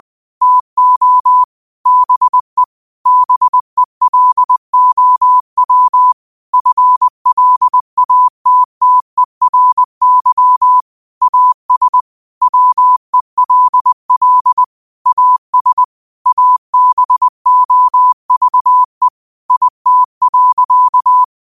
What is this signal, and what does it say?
Morse code, heard as a keyed tone, about 20 wpm, 1000 Hz. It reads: TO BE BELOW FLATTERY AS WELL AS ABOVE IT.
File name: qotdfiles / 20wpm